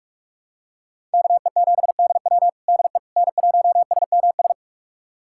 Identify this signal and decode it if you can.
Morse code, heard as a keyed tone, about 42 wpm, 700 Hz. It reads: KE7DW DE N1SMS
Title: USA Callsign DE Callsign - 5 Characters - 45wpm